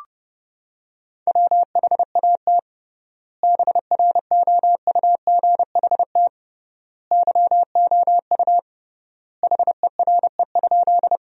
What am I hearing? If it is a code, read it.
Morse code, heard as a keyed tone, about 32 wpm, 700 Hz. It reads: WHAT BROUGHT YOU HERE?
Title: 3x Word-Spacing - Sentences from Top 500 Words - 30wpm